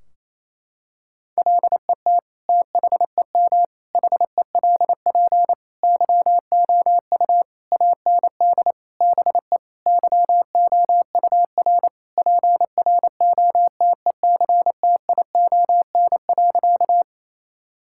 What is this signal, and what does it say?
Morse code, heard as a keyed tone, about 28 wpm, 700 Hz. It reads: LET THEM HELP YOU AND BE YOUR PROTECTION.
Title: Sentences from Top 1600 Words 28wpm